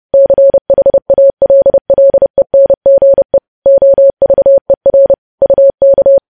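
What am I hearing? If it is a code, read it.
Morse code, heard as a keyed tone, about 30 wpm, 550 Hz. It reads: CHALLENGE OVER UK
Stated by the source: News headlines sent in Morse code at 30.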